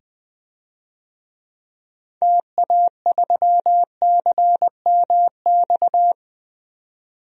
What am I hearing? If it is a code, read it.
Morse code, heard as a keyed tone, about 20 wpm, 700 Hz. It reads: TA3CMX
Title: DXCC-100 Call Signs - 6 Characters - 20wpm